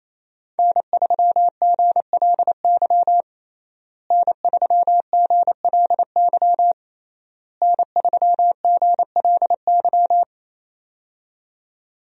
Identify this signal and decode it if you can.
Morse code, heard as a keyed tone, about 28 wpm, 700 Hz. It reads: N3GLY N3GLY N3GLY
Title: USA Call Signs - 5 Characters - Repeated 3x - 28wpm